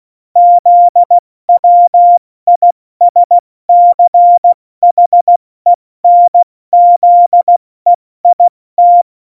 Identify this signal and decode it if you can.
Morse code, heard as a keyed tone, about 16 wpm, 700 Hz. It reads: ZWISCHENZEIT
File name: Morsekurs / Witze / Witz 16. Januar 2018, 70 F80